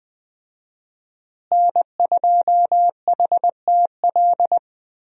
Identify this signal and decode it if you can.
Morse code, heard as a keyed tone, about 21 wpm, 700 Hz. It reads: N2HTL
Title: USA Call Signs - 5 Characters - Repeated 3x - 20wpm